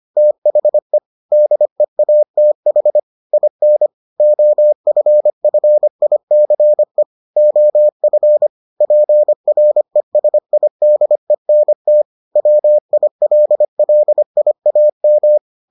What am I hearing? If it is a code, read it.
Morse code, heard as a keyed tone, about 25 wpm, 600 Hz. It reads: THE DEATH IN OFFICE OF PRESIDENT WILLIAM